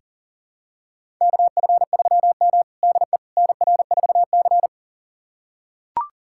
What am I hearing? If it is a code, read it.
Morse code, heard as a keyed tone, about 40 wpm, 700 Hz. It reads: KF3M DE NR4C
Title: USA Callsign DE Callsign - 4 Characters - 40wpm